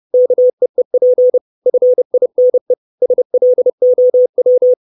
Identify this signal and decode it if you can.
Morse code, heard as a keyed tone, about 30 wpm, 500 Hz. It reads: KEEP FINE SLOW
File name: 632 Words 30WPM